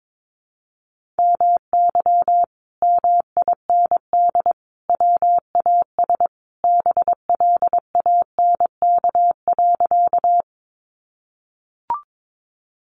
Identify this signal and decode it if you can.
Morse code, heard as a keyed tone, about 22 wpm, 700 Hz. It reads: MY MIND WAS BLANK.
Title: Sentences from Top 1500 Words 22wpm